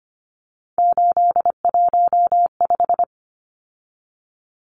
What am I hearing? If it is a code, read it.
Morse code, heard as a keyed tone, about 25 wpm, 700 Hz. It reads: 815
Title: Numbers - 3-Digit 25wpm